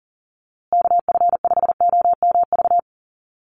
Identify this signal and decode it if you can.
Morse code, heard as a keyed tone, about 40 wpm, 700 Hz. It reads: KF5OMV
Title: USA Call Signs Repeated 2x 40wpm